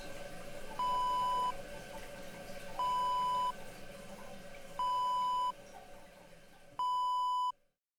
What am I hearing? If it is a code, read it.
There is no code.